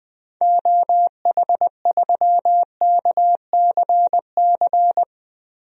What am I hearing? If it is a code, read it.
Morse code, heard as a keyed tone, about 20 wpm, 700 Hz. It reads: OH3KCC
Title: DXCC-100 Call Signs - 6 Characters - Repeated 3x - 20wpm